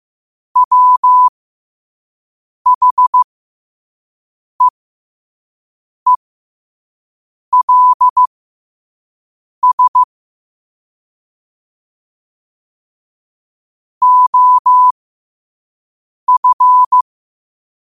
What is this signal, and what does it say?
Morse code, heard as a keyed tone, about 15 wpm, spaced as for 5 wpm, 1000 Hz. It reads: WHEELS OF
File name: qotdfiles / 5wpm